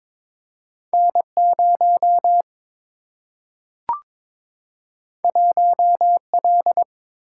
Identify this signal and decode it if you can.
Morse code, heard as a keyed tone, about 22 wpm, 700 Hz. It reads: N0 1L
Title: Single-Letter Single-Number 22wpm